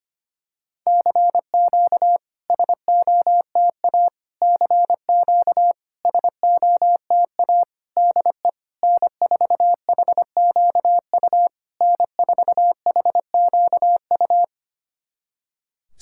Morse code, heard as a keyed tone, about 25 wpm, 700 Hz. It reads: CQ SOTA CQ SOTA DE N4HQU N4HQU
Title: SOTA QSO Practice 25wpm